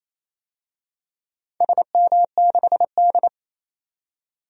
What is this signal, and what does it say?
Morse code, heard as a keyed tone, about 28 wpm, 700 Hz. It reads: SM6D
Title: DXCC-100 Call Signs - 4 Characters - 28wpm